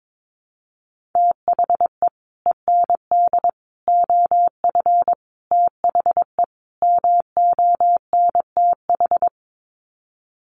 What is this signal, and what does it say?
Morse code, heard as a keyed tone, about 22 wpm, 700 Hz. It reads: THE END OF THE MONTH
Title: Sets of 5 Words Encore 22wpm